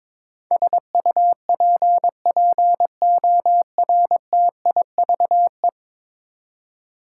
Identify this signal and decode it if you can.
Morse code, heard as a keyed tone, about 22 wpm, 700 Hz. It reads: SUPPORTIVE
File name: practice / 22wpm / 10 Letter Words - First 3 Letters Given - 22wpm